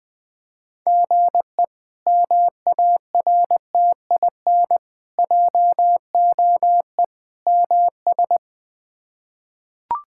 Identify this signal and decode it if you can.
Morse code, heard as a keyed tone, about 20 wpm, 700 Hz. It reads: GE MARTIN JOE MS
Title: K1USN SST QSO Practice - 20wpm